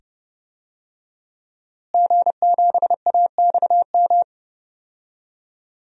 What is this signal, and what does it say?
Morse code, heard as a keyed tone, about 30 wpm, 700 Hz. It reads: G7AXM
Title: DXCC-100 Call Signs - 5 Characters - Repeated 2x - 30wpm